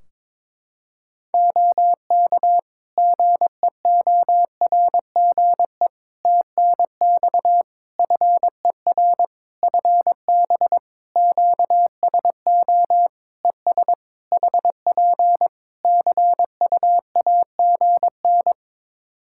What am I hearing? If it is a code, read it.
Morse code, heard as a keyed tone, about 22 wpm, 700 Hz. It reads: OK GEORGE TNX FER FB QSO ES HP CUAGN
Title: K1USN SST QSO Practice 22wpm